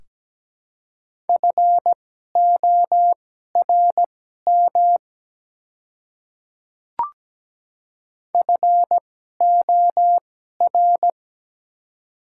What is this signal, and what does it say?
Morse code, heard as a keyed tone, about 17 wpm, spaced as for 12 wpm, 700 Hz. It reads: FORM FOR
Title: 2x Character Spacing - Top 200 Words - 17wpm